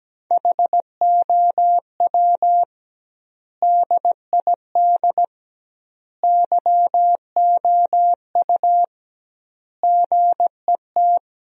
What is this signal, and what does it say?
Morse code, heard as a keyed tone, about 17 wpm, 700 Hz. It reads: HOW DID YOU GET
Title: 2x Word-Spacing - Sentences from Top 200 Words - 17wpm